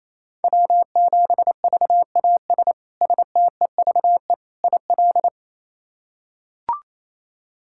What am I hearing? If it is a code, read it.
Morse code, heard as a keyed tone, about 28 wpm, 700 Hz. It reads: W7VAS STEVE IL
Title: K1USN SST QSO Practice - 28wpm